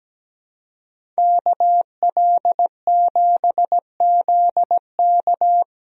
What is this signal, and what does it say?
Morse code, heard as a keyed tone, about 17 wpm, 700 Hz. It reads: KL7ZK
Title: USA Call Signs - 5 Characters - Repeated 3x - 17wpm